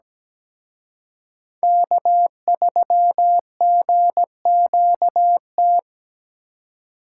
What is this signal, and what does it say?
Morse code, heard as a keyed tone, about 20 wpm, 700 Hz. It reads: K3GQT
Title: Speed Racing - USA 1x3 Call Signs 17wpm